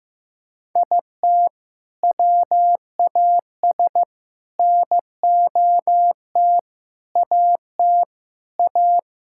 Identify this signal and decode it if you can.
Morse code, heard as a keyed tone, about 15 wpm, 700 Hz. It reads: IT WAS NOT AT A
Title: Sentences from Top 1000 Words - Review Part 5 - 15wpm